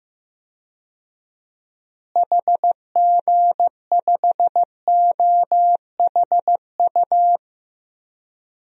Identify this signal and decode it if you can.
Morse code, heard as a keyed tone, about 15 wpm, 700 Hz. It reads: HG5OHU
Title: DXCC-100 Call Signs - 6 Characters - Repeated 3x - 15wpm